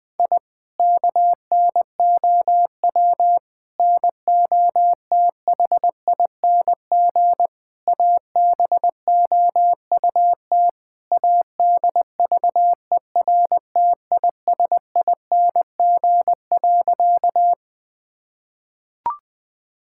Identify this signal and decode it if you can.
Morse code, heard as a keyed tone, about 20 wpm, 700 Hz. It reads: I KNOW NOTHING ABOUT ADVERTISING.
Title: Sentences from Top 1000 Words - Review Part 6 - 20wpm